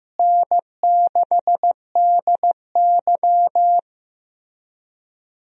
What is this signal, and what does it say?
Morse code, heard as a keyed tone, about 15 wpm, 700 Hz. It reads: N6DY
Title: USA Call Signs - 4 Characters - Repeated 3x - 15wpm